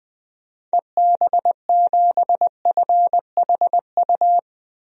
Morse code, heard as a keyed tone, about 20 wpm, 700 Hz. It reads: EB7FHU
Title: DXCC-100 Call Signs - 6 Characters - 20wpm